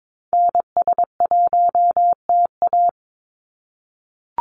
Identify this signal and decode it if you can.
Morse code, heard as a keyed tone, about 22 wpm, 700 Hz. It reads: NS1TA
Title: NA SOTA Activator Callsigns 22wpm